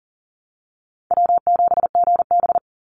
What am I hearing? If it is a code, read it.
Morse code, heard as a keyed tone, about 40 wpm, 700 Hz. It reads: W7GB
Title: Call Signs - 40wpm